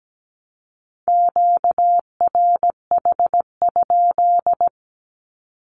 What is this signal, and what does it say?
Morse code, heard as a keyed tone, about 17 wpm, 700 Hz. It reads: QRH?
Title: Q-Codes 17wpm